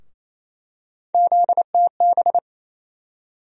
Morse code, heard as a keyed tone, about 28 wpm, 700 Hz. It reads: ZTB
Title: USA Call Sign Postfixes - 3 Letters 28wpm